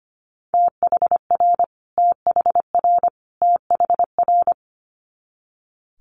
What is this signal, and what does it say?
Morse code, heard as a keyed tone, about 25 wpm, 700 Hz. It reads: THR THR THR
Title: Multi-letter Phonemes Repeated 3x 25wpm